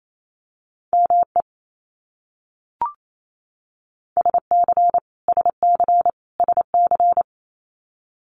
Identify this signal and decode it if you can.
Morse code, heard as a keyed tone, about 28 wpm, 700 Hz. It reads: ME SC SC SC
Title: US States Repeated 3x 28wpm